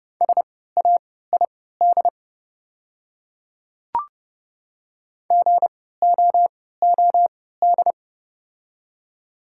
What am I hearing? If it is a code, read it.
Morse code, heard as a keyed tone, about 30 wpm, spaced as for 17 wpm, 690 Hz. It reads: SAID GOOD
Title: 3x Character Spacing - Top 200 Words - 30wpm